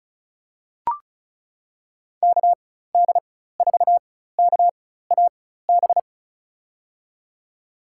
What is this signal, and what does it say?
Morse code, heard as a keyed tone, about 36 wpm, spaced as for 16 wpm, 700 Hz. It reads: KD4KAB
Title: USA Call Signs - 4x Character Spacing - 35wpm